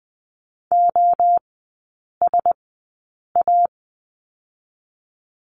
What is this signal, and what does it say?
Morse code, heard as a keyed tone, about 20 wpm, spaced as for 8 wpm, 700 Hz. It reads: OSA
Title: 035 Lesson 7e - Letter s - Three Characters 20wpm